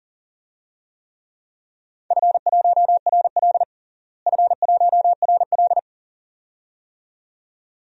Morse code, heard as a keyed tone, about 40 wpm, 700 Hz. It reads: F1RL F1RL
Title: DXCC-100 Call Signs - 4 Characters - Repeated 2x - 40wpm